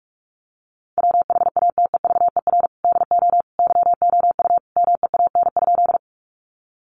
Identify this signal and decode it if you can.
Morse code, heard as a keyed tone, about 44 wpm, 700 Hz. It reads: WHATEVER DO YOU MEAN?
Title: Sentences from Top 1100 Words 45wpm